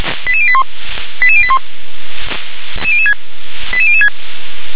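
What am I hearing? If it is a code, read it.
Five-tone selective call (ZVEI), heard as quick runs of five tones, tones of about 70 ms. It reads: 88081 88081 88006
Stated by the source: ZVEI five-tone call received on SDR.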